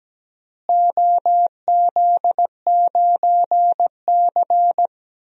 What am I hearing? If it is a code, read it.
Morse code, heard as a keyed tone, about 17 wpm, 700 Hz. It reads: OZ9C
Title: DXCC-100 Call Signs - 4 Characters - Repeated 2x - 17wpm